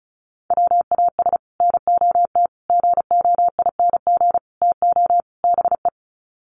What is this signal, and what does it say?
Morse code, heard as a keyed tone, about 34 wpm, 700 Hz. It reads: WAS NOT GOING TO BE
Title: Sets of 5 Words Encore 35wpm